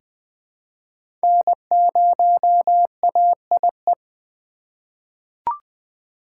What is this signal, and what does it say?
Morse code, heard as a keyed tone, about 20 wpm, 700 Hz. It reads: N0AIE
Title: K1USN SST QSO Practice - 20wpm